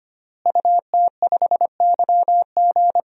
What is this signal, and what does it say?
Morse code, heard as a keyed tone, about 25 wpm, 700 Hz. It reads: UT5YG
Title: DXCC-100 Call Signs - 5 Characters - Repeated 3x - 25wpm